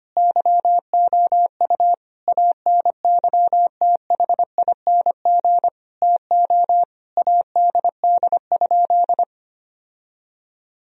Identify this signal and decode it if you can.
Morse code, heard as a keyed tone, about 25 wpm, 700 Hz. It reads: YOU ANYTHING TO ADD?
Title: Sentences from Top 1000 Words - Review Part 1 - 25wpm